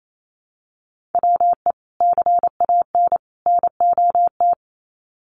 Morse code, heard as a keyed tone, about 28 wpm, 700 Hz. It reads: WE CAN NOT
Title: Sets of 3 Words 28wpm